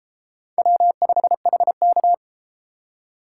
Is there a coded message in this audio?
Yes: Morse code from 0.6 to 2.1 s, heard as a keyed tone, about 31 wpm, 700 Hz. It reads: W5HK